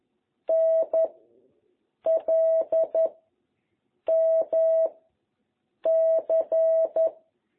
Morse code, heard as a keyed tone, about 11 wpm, 660 Hz. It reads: N L M C